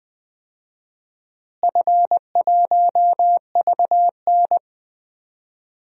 Morse code, heard as a keyed tone, about 20 wpm, 700 Hz. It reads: F1VN
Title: DXCC-100 Call Signs - 4 Characters - Repeated 3x - 20wpm